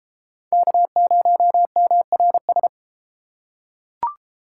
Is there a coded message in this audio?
Yes: Morse code from 0.5 to 2.7 s, heard as a keyed tone, about 30 wpm, 700 Hz. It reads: K0MRS